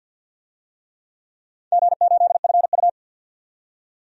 Morse code, heard as a keyed tone, about 51 wpm, 700 Hz. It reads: G8FU
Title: DXCC-100 Call Signs - 4 Characters - 50wpm